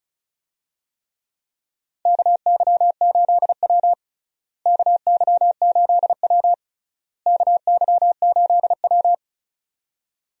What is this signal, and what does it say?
Morse code, heard as a keyed tone, about 34 wpm, 700 Hz. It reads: KY8W KY8W KY8W
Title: USA Call Signs - 4 Characters - Repeated 3x - 35wpm